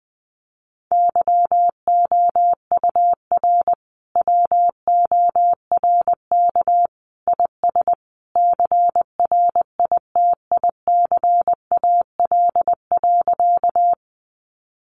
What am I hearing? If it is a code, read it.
Morse code, heard as a keyed tone, about 20 wpm, 700 Hz. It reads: YOUR WORK IS CRITICAL.
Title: Sentences from Top 1100 Words 20wpm